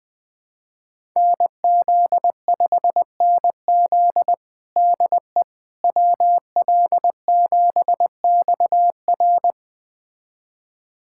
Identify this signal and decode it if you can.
Morse code, heard as a keyed tone, about 20 wpm, 700 Hz. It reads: NZ5NZ DE WL7XR
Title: USA Callsign DE Callsign - 5 Characters - 20wpm